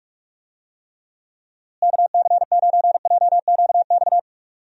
Morse code, heard as a keyed tone, about 41 wpm, 700 Hz. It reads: KC9JQX
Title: USA Call Signs - 6 Characters - Repeated 3x - 45wpm